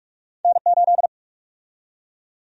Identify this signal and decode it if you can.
Morse code, heard as a keyed tone, about 44 wpm, 700 Hz. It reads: N8
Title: USA Call Sign Prefixes - 45wpm